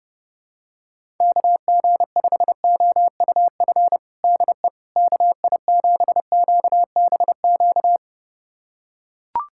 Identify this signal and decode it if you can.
Morse code, heard as a keyed tone, about 30 wpm, 700 Hz. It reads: KG5OUF DE KI7QBQ
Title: USA Callsign DE Callsign 30wpm